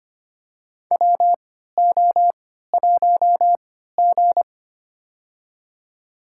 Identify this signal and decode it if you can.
Morse code, heard as a keyed tone, about 25 wpm, spaced as for 14 wpm, 700 Hz. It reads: WO1G
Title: USA Call Signs - 4 Characters - 3x Character Spacing - 25wpm